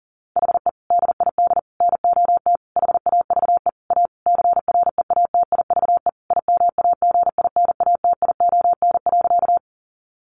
Morse code, heard as a keyed tone, about 40 wpm, 700 Hz. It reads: HE DID NOT HAVE A CREATIVE IMAGINATION.
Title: Sentences from Top 1300 Words 40wpm